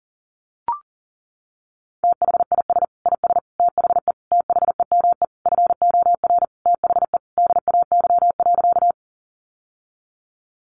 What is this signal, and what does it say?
Morse code, heard as a keyed tone, about 40 wpm, 700 Hz. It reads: THIS IS THE THEME FOR THE DAY.
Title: Sentences from Top 1100 Words 40wpm